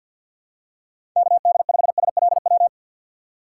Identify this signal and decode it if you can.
Morse code, heard as a keyed tone, about 50 wpm, 700 Hz. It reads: KD5SLW
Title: USA Call Signs - 6 Characters - 4x Character Spacing - 50wpm